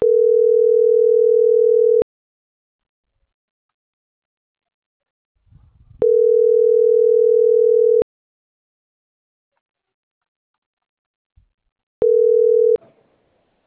Ringback tone (North America), 440 + 480 Hz.